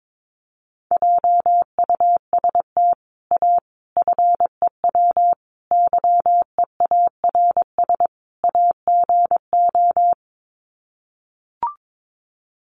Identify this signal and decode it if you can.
Morse code, heard as a keyed tone, about 22 wpm, 700 Hz. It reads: JUST A FEW YEARS AGO
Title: Sets of 5 Words Encore 22wpm